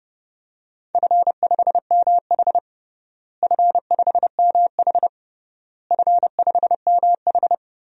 Morse code, heard as a keyed tone, about 30 wpm, 700 Hz. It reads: F5MH F5MH F5MH
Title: DXCC-100 Call Signs - 4 Characters - Repeated 3x - 30wpm